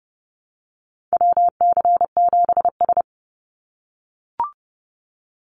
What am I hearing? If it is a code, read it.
Morse code, heard as a keyed tone, about 30 wpm, 700 Hz. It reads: WC7S
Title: Call Signs - 30wpm